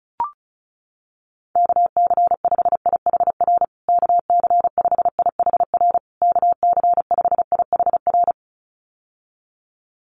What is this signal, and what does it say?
Morse code, heard as a keyed tone, about 34 wpm, 700 Hz. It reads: KC5IHR KC5IHR KC5IHR
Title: USA Call Signs Repeated 3x 35wpm